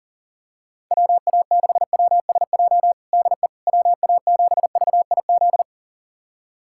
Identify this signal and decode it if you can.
Morse code, heard as a keyed tone, about 40 wpm, 700 Hz. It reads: WA6WSJ DE WA7VIZ
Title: USA Callsign DE Callsign - 6 Characters - 40wpm